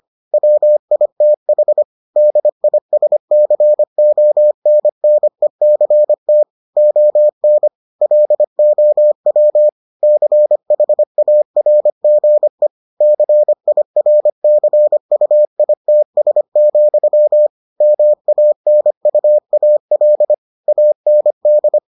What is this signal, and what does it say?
Morse code, heard as a keyed tone, about 25 wpm, 600 Hz. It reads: WITH DISCONNECT ON LOW CHARGE CIRCUITS, MANUAL AND